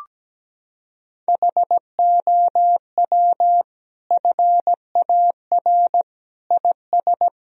Morse code, heard as a keyed tone, about 17 wpm, 700 Hz. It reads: HOW FAR IS
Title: Sentences from Top 1000 Words - Review Part 1 - 17wpm